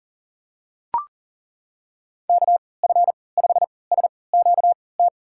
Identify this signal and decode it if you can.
Morse code, heard as a keyed tone, about 38 wpm, spaced as for 22 wpm, 700 Hz. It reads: KF5SQT
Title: USA Call Signs - 3x Character Spacing - 40wpm